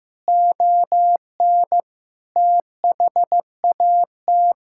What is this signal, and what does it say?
Morse code, heard as a keyed tone, about 15 wpm, 700 Hz. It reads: ON THAT
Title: Sentences from Top 1600 Words 15wpm